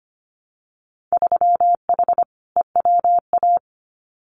Morse code, heard as a keyed tone, about 25 wpm, 700 Hz. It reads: 3H EWA
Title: Winter Field Day Exchanges Encore 25wpm